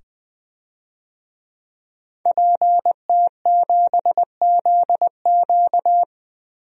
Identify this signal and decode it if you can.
Morse code, heard as a keyed tone, about 20 wpm, 700 Hz. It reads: PT7ZQ
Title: DXCC-100 Call Signs - 5 Characters - 20wpm